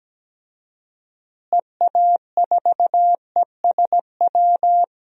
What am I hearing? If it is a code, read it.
Morse code, heard as a keyed tone, about 17 wpm, 700 Hz. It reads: EA4ESW
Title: DXCC-100 Call Signs - 6 Characters - Repeated 2x - 17wpm